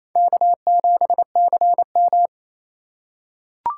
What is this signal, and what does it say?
Morse code, heard as a keyed tone, about 28 wpm, 700 Hz. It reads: K7CM
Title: USA Call Signs - 4 Characters - 3x Character Spacing - 28wpm